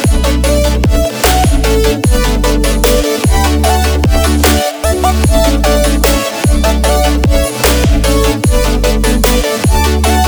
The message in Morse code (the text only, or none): none